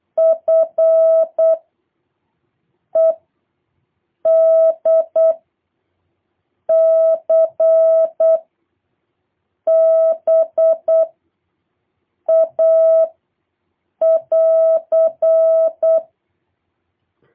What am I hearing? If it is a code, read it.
Morse code, heard as a keyed tone, about 8 wpm, 650 Hz. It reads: F E D C B A <AR>